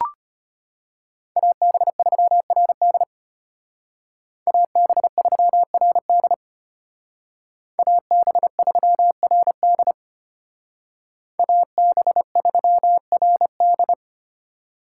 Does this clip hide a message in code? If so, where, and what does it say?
Yes: Morse code from 1.4 to 13.9 s, heard as a keyed tone, about 33 wpm, 700 Hz. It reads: AB3RD AB3RD AB3RD AB3RD